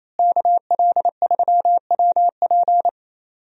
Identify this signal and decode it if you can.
Morse code, heard as a keyed tone, about 28 wpm, 700 Hz. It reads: KL3WP
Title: USA Call Signs - 5 Characters - Repeated 2x - 28wpm